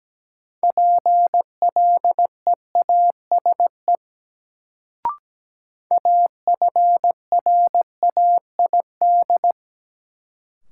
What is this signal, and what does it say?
Morse code, heard as a keyed tone, about 17 wpm, 700 Hz. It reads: PLEASE AFRAID
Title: Sentences from Top 1500 Words - New Word Drill - 17wpm